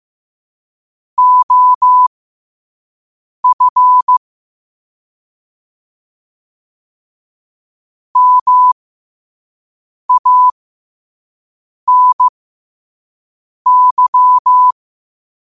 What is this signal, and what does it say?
Morse code, heard as a keyed tone, about 15 wpm, spaced as for 5 wpm, 1000 Hz. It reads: OF MANY
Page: New quotes every day in morse code at 5 Words per minute.